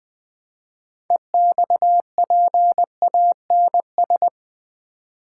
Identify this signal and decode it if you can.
Morse code, heard as a keyed tone, about 20 wpm, 700 Hz. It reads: EXPANS
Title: Build A Word - 3 Letters At A Time - 9-Letter Words - 20wpm